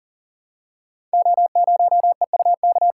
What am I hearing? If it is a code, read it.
Morse code, heard as a keyed tone, about 42 wpm, 690 Hz. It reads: O0EUK
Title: Code Groups - 5 Letter:Numbers - 40wpm